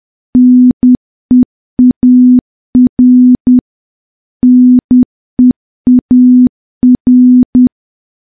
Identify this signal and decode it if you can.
Morse code, heard as a keyed tone, about 10 wpm, 250 Hz. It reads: NEAR NEAR
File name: morse / morse-training-10WPM